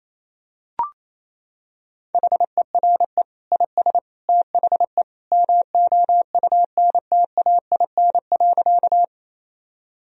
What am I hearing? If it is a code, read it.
Morse code, heard as a keyed tone, about 28 wpm, 700 Hz. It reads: HERE IS THE MOUNTAIN.
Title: Sentences from Top 1000 Words - Review Part 1 - 28wpm